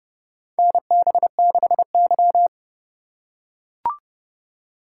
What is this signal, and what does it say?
Morse code, heard as a keyed tone, about 32 wpm, 700 Hz. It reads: NB6Y
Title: USA Call Signs - 4 Characters - 4x Character Spacing - 30wpm